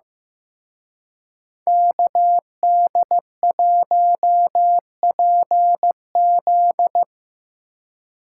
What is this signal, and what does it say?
Morse code, heard as a keyed tone, about 17 wpm, 700 Hz. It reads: KD1PZ
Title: Speed Racing - USA 2x2 Call Signs 15wpm